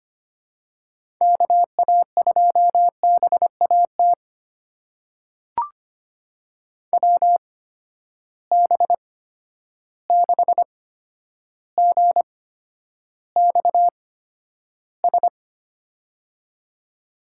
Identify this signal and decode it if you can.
Morse code, heard as a keyed tone, about 25 wpm, spaced as for 11 wpm, 700 Hz. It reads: KA2BAT WB6GXS
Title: USA Call Signs - 6 Characters - 8x Character Spacing - 25wpm